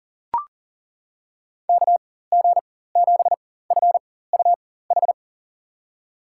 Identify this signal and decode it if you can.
Morse code, heard as a keyed tone, about 40 wpm, spaced as for 19 wpm, 700 Hz. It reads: KG7FUH
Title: USA Call Signs - 6 Characters - 4x Character Spacing - 40wpm